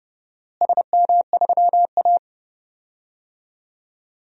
Morse code, heard as a keyed tone, about 30 wpm, 700 Hz. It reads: SM3A
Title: DXCC-100 Call Signs - 4 Characters - Repeated 2x - 30wpm